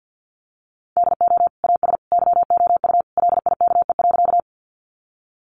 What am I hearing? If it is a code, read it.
Morse code, heard as a keyed tone, about 50 wpm, 700 Hz. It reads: DO AS YOU LIKE.
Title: Sentences from Top 100 Words 50wpm